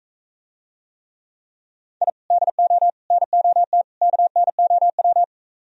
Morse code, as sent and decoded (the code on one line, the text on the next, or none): .. / -.. --- / -. --- - / -.- -. --- .--
I DO NOT KNOW